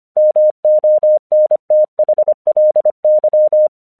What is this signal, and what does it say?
Morse code, heard as a keyed tone, about 25 wpm, 610 Hz. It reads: MONTHLY
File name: 1 English 25wpm / AQC07